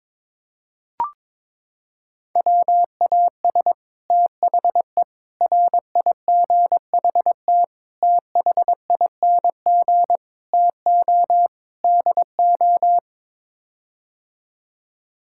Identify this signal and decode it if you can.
Morse code, heard as a keyed tone, about 22 wpm, 700 Hz. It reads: WAS THE RIGHT THING TO DO
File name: practice / 22wpm / Sets of 6 Words - 22wpm